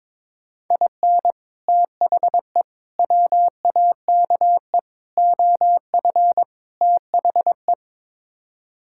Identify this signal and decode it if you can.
Morse code, heard as a keyed tone, about 22 wpm, 700 Hz. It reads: IN THE WAKE OF THE
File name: practice / 22wpm / Build A Phrase - 5-Word Phrases - 22wpm